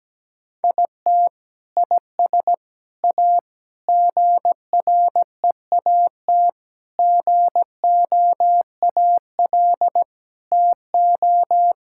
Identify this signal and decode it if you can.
Morse code, heard as a keyed tone, about 17 wpm, 700 Hz. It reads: IT IS A GREAT GOAL TO
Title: Sentences from Top 1000 Words - Review Part 5 - 17wpm